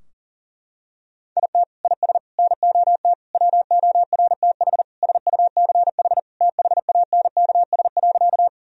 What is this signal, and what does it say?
Morse code, heard as a keyed tone, about 40 wpm, 700 Hz. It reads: IT IS NOT WORTH SUCH THANKS.
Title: Sentences from Top 1000 Words - Review Part 3 - 40wpm